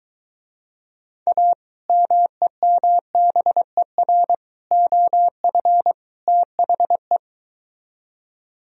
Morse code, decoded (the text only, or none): A MEMBER OF THE